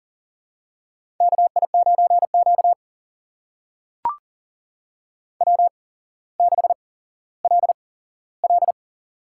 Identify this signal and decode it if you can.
Morse code, heard as a keyed tone, about 40 wpm, spaced as for 17 wpm, 700 Hz. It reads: KI9Q W6LL
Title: USA Call Signs - 4 Characters - 8x Character Spacing - 40wpm